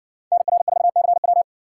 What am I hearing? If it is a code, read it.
Morse code, heard as a keyed tone, about 60 wpm, 700 Hz. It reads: NR4CW